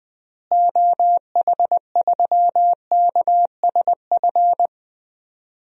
Morse code, heard as a keyed tone, about 20 wpm, 700 Hz. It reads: OH3KSF
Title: DXCC-100 Call Signs - 6 Characters - Repeated 2x - 20wpm